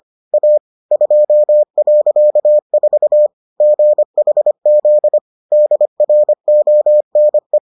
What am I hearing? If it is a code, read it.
Morse code, heard as a keyed tone, about 25 wpm, 600 Hz. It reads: A 2.4 GHZ DRONE